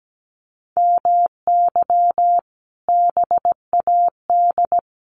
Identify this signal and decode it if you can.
Morse code, heard as a keyed tone, about 17 wpm, 700 Hz. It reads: MY BAD
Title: Sentences from Top 1500 Words 17wpm